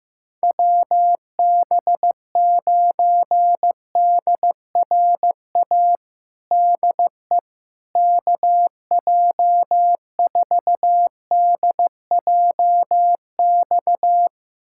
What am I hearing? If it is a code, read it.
Morse code, heard as a keyed tone, about 16 wpm, 700 Hz. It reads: WB9DRA DE KJ4DJX
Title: USA Callsign DE Callsign - 6 Characters - 15wpm